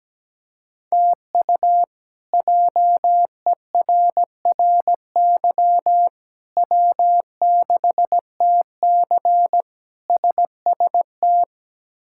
Morse code, heard as a keyed tone, about 17 wpm, 700 Hz. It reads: TU JERRY W6TC SST
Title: K1USN SST QSO Practice - 17wpm